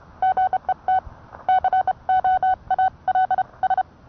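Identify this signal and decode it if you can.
Morse code, heard as a keyed tone, about 29 wpm, spaced as for 25 wpm, 750 Hz. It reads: GET COALS
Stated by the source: Morse code sent by N1EA on a Vibroplex key.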